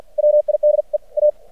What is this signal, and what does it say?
Morse code, heard as a keyed tone, about 52 wpm, 600 Hz. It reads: WID E A
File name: archived / CRSnet